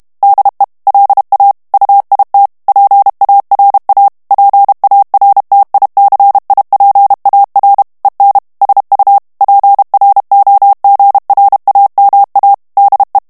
Morse code, heard as a keyed tone, about 32 wpm, 800 Hz. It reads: DE LA UIT PARA PARTICIPAR EN SU PROGRAMA DE
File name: morse mp3 / ESPAGNOL TEXTE EN CLAIR 1 VITESSE 30 MOTS MINUTES